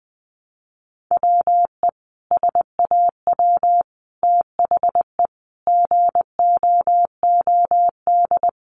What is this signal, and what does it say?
Morse code, heard as a keyed tone, about 20 wpm, 700 Hz. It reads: WE SAW THE GOOD
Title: Sentences from Top 1100 Words 20wpm